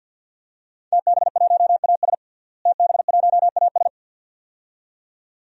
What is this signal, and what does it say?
Morse code, heard as a keyed tone, about 52 wpm, 700 Hz. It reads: TB1AS TB1AS
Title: DXCC-100 Call Signs - 5 Characters - Repeated 2x - 50wpm